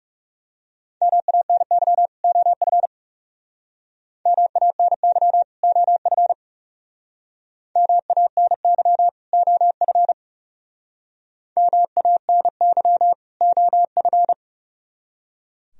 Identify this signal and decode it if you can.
Morse code, heard as a keyed tone, about 38 wpm, 700 Hz. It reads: MANY OF MANY OF MANY OF MANY OF